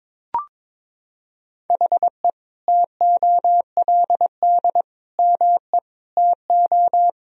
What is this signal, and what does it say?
Morse code, heard as a keyed tone, about 21 wpm, 700 Hz. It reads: HE TOLD ME TO
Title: Sentences from Top 1000 Words - Review Part 3 - 22wpm